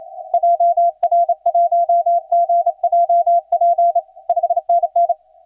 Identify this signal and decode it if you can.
Morse code, heard as a keyed tone, about 29 wpm, 690 Hz. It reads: JR1GJP 5NN
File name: s9yy-15cw